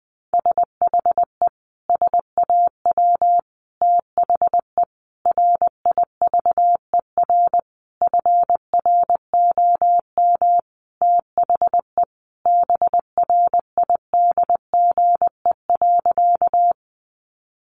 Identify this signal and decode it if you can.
Morse code, heard as a keyed tone, about 19 wpm, 700 Hz. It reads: SHE SAW THE RIVER FROM THE BRIDGE.
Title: Sentences from Top 1200 Words 20wpm